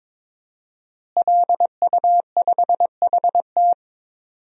Morse code, heard as a keyed tone, about 22 wpm, 700 Hz. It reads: LU5HT
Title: DXCC-100 Call Signs - 5 Characters - Repeated 2x - 22wpm